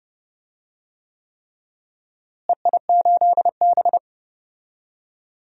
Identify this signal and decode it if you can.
Morse code, heard as a keyed tone, about 30 wpm, 700 Hz. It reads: EI8B
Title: DXCC-100 Call Signs - 4 Characters - Repeated 3x - 30wpm